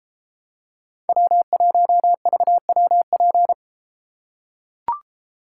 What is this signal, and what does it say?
Morse code, heard as a keyed tone, about 32 wpm, 700 Hz. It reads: W1VWP